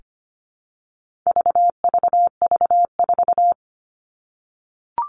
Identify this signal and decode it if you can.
Morse code, heard as a keyed tone, about 25 wpm, 700 Hz. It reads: VVV4
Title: All Cut Numbers - 4-Digit 25wpm